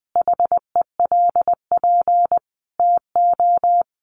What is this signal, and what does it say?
Morse code, heard as a keyed tone, about 20 wpm, 700 Hz. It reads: HELP TO
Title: Sentences from Top 1500 Words 20wpm